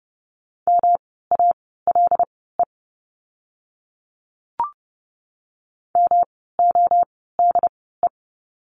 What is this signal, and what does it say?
Morse code, heard as a keyed tone, about 30 wpm, spaced as for 17 wpm, 700 Hz. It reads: MALE MODE
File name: course / 30:10wpm / 108 Lesson 18f - Letter m - Words 30wpm